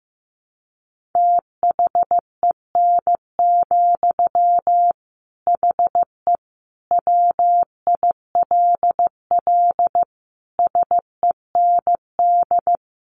Morse code, heard as a keyed tone, about 15 wpm, 700 Hz. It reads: THEN, HE WILL SEND
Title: Sentences from Top 1500 Words 15wpm